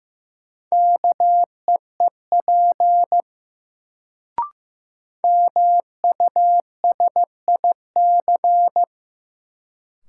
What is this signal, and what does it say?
Morse code, heard as a keyed tone, about 15 wpm, 700 Hz. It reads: KEEP MUSIC
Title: Sentences from Top 300 Words - New Word Drill - 15wpm